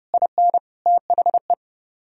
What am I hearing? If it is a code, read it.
Morse code, heard as a keyed tone, about 34 wpm, 700 Hz. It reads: IN THE